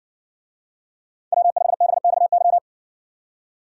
Morse code, heard as a keyed tone, about 60 wpm, 700 Hz. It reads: W5BXY